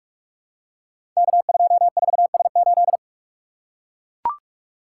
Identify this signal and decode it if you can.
Morse code, heard as a keyed tone, about 44 wpm, 700 Hz. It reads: K24S8
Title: Code Groups - 5 Letter:Numbers - 45wpm